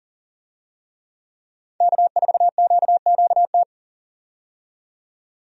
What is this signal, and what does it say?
Morse code, heard as a keyed tone, about 40 wpm, 700 Hz. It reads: K4QQT
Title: USA Call Signs - 5 Characters - 40wpm